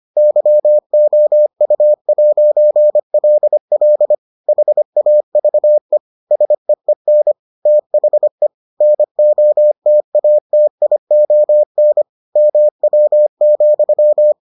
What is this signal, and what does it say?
Morse code, heard as a keyed tone, about 25 wpm, 600 Hz. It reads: YOU'LL HAVE SEEN THE NOTATION MW,